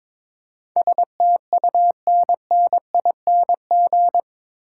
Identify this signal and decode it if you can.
Morse code, heard as a keyed tone, about 21 wpm, 700 Hz. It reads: STUNNING
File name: practice / 22wpm / Eight Letter Words - 22wpm